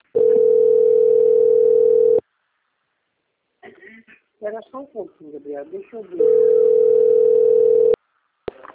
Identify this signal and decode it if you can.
Ringback tone (North America), 440 + 480 Hz, with speech.
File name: out- - - - - .29112